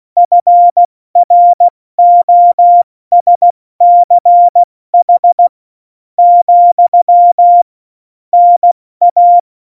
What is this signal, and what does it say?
Morse code, heard as a keyed tone, about 16 wpm, spaced as for 13 wpm, 700 Hz. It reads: FROSCH, NA
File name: Morsekurs / Witze / Witz 11. September 2018, 70 F80